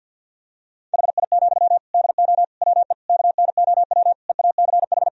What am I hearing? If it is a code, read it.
Morse code, heard as a keyed tone, about 51 wpm, 700 Hz. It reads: HI, DO WE KNOW EACH
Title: Sentences from Top 1000 Words - Review Part 3 - 50wpm